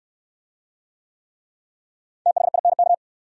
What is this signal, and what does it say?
Morse code, heard as a keyed tone, about 68 wpm, 700 Hz. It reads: THINK